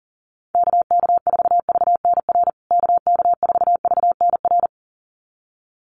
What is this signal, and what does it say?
Morse code, heard as a keyed tone, about 40 wpm, 700 Hz. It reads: KK4VNR KK4VNR
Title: USA Call Signs Repeated 2x 40wpm